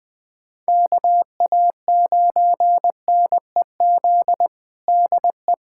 Morse code, heard as a keyed tone, about 20 wpm, 700 Hz. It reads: KA9NEZ DE
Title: USA Callsign DE Callsign 20wpm